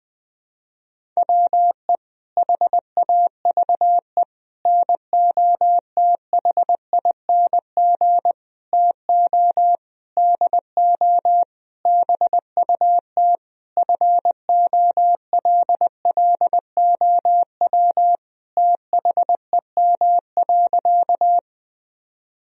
Morse code, heard as a keyed tone, about 20 wpm, 700 Hz. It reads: WE HAVE NOTHING TO DO BUT FOLLOW THEM.
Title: Sentences from Top 1000 Words - Review Part 1 - 20wpm